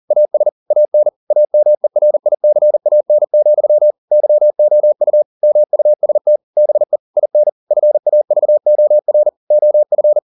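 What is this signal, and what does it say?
Morse code, heard as a keyed tone, about 40 wpm, 600 Hz. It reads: AS AN AMERICAN, YOU MUST BE IN FAVOR OF
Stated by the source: Morse code at 40 wpm, practice text, CW med Gnister.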